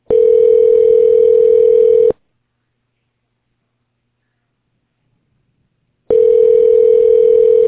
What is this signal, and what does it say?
Ringback tone (North America), 440 + 480 Hz.